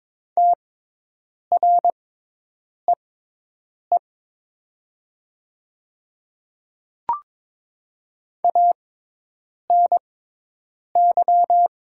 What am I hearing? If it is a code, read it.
Morse code, heard as a keyed tone, about 22 wpm, spaced as for 8 wpm, 700 Hz. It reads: TREE ANY
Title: 6x Character Spacing - Top 500 Words - 22wpm